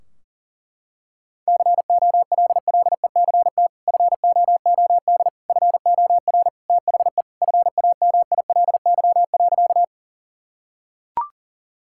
Morse code, heard as a keyed tone, about 37 wpm, 700 Hz. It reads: COLLECT FOOD FOR THE FAMILY.
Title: Sentences from Top 1600 Words 40wpm